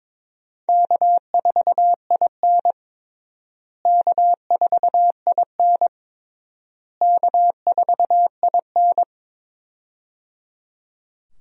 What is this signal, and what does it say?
Morse code, heard as a keyed tone, about 22 wpm, 700 Hz. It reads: K4IN K4IN K4IN
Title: USA Call Signs - 4 Characters - Repeated 3x - 22wpm